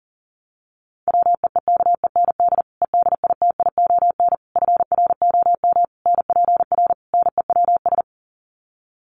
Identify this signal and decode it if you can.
Morse code, heard as a keyed tone, about 40 wpm, 700 Hz. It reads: WEEKEND EDITION FROM NPR NEWS
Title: Sets of 5 Words 40wpm